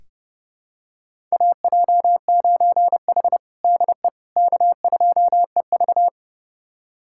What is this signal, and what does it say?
Morse code, heard as a keyed tone, about 30 wpm, 700 Hz. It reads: AJ9H DE K2EV
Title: USA Callsign DE Callsign - 4 Characters - 30wpm